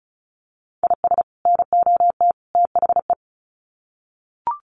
Morse code, heard as a keyed tone, about 34 wpm, 700 Hz. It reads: IS NOT THE
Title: Sets of 3 Words 35wpm